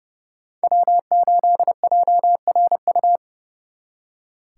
Morse code, heard as a keyed tone, about 38 wpm, 700 Hz. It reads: W8JRU